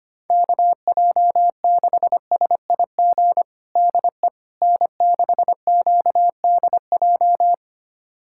Morse code, heard as a keyed tone, about 25 wpm, 700 Hz. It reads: KJ6SIG DE N6QDJ
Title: USA Callsign DE Callsign 25wpm